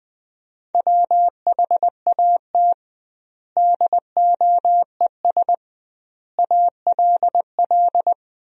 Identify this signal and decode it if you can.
Morse code, heard as a keyed tone, about 20 wpm, 700 Hz. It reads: WHAT DOES ALL
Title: 2x Word-Spacing - Sentences from Top 200 Words - 20wpm